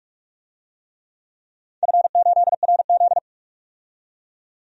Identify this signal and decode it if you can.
Morse code, heard as a keyed tone, about 44 wpm, 690 Hz. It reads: F8RZ
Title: DXCC-100 Call Signs - 4 Characters - 45wpm